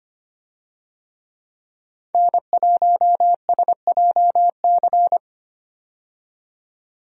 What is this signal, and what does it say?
Morse code, heard as a keyed tone, about 25 wpm, 700 Hz. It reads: N1SJC
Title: USA Call Signs - 5 Characters - 25wpm